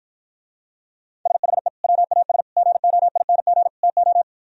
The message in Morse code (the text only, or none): SHE WAS GOING TO